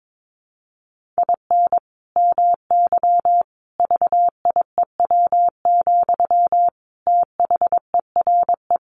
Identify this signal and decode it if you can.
Morse code, heard as a keyed tone, about 22 wpm, 700 Hz. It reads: IN MY VIEW, THERE
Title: Sentences from Top 1500 Words 22wpm